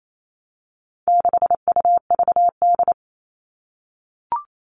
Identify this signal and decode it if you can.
Morse code, heard as a keyed tone, about 28 wpm, 700 Hz. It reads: 6UVD
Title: All Cut Numbers - 4-Digit 28wpm